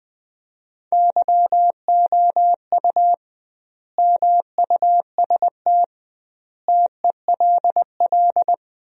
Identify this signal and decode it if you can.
Morse code, heard as a keyed tone, about 20 wpm, 700 Hz. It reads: YOU MUST TELL
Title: 2x Word-Spacing - Sentences from Top 200 Words - 20wpm